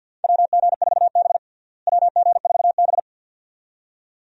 Morse code, heard as a keyed tone, about 51 wpm, 700 Hz. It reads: WG4B WG4B
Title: USA Call Signs - 4 Characters - Repeated 2x - 50wpm